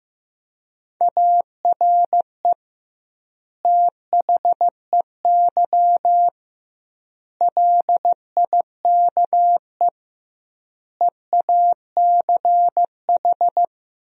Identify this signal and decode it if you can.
Morse code, heard as a keyed tone, about 15 wpm, 700 Hz. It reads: ARE THEY LIKE EACH
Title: 2x Word-Spacing - Sentences from Top 200 Words - 15wpm